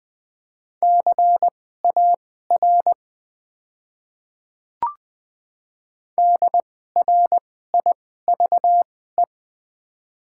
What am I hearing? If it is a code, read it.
Morse code, heard as a keyed tone, about 21 wpm, spaced as for 15 wpm, 690 Hz. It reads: CAR DRIVE
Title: 2x Character Spacing - Top 500 Words - 20wpm